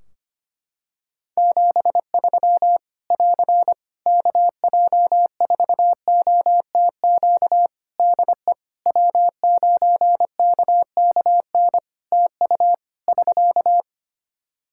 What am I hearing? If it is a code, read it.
Morse code, heard as a keyed tone, about 25 wpm, 700 Hz. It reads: 73 <AR> KJ4OTQ DE W9KKN TU <SK>
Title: Practice QSO - Exchange 4 - Ending - 25wpm